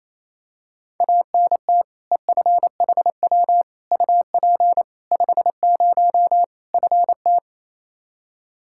Morse code, heard as a keyed tone, about 28 wpm, 700 Hz. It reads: ANT EFHW UP 50 FT
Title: Practice QSO - Exchange 2 - Info - 28wpm